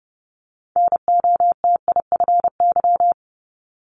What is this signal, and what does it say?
Morse code, heard as a keyed tone, about 30 wpm, 700 Hz. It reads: NOTIFY
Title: Multi-letter Phonemes 30wpm